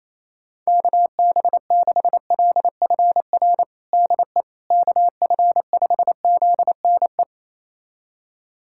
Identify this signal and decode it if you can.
Morse code, heard as a keyed tone, about 28 wpm, 700 Hz. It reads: KB6LFR DE KF5ZNE
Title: USA Callsign DE Callsign 28wpm